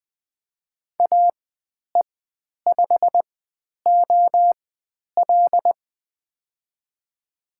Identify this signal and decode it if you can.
Morse code, heard as a keyed tone, about 20 wpm, spaced as for 10 wpm, 700 Hz. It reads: AE5OL
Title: 095 Lesson 16g - Letter c - Call Signs 20wpm